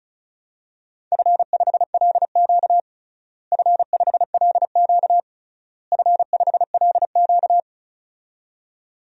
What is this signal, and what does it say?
Morse code, heard as a keyed tone, about 34 wpm, 700 Hz. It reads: F5LQ F5LQ F5LQ
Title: DXCC-100 Call Signs - 4 Characters - Repeated 3x - 35wpm